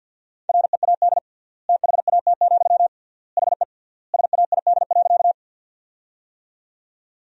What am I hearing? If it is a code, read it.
Morse code, heard as a keyed tone, about 46 wpm, 700 Hz. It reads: READ THAT, HE SAID.
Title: 3x Word-Spacing - Sentences from Top 200 Words - 50wpm